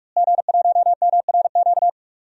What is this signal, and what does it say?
Morse code, heard as a keyed tone, about 42 wpm, 700 Hz. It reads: G1MRQ
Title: DXCC-100 Call Signs - 5 Characters - Repeated 2x - 45wpm